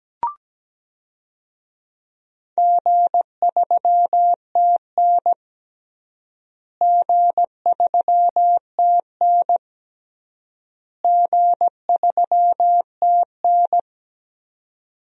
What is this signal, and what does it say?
Morse code, heard as a keyed tone, about 17 wpm, 700 Hz. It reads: G3TN G3TN G3TN
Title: DXCC-100 Call Signs - 4 Characters - Repeated 3x - 17wpm